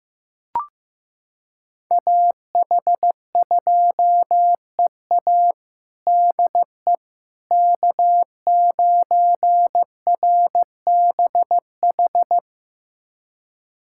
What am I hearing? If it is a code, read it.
Morse code, heard as a keyed tone, about 15 wpm, 700 Hz. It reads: AH2EA DE K9RBH
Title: USA Callsign DE Callsign - 5 Characters - 15wpm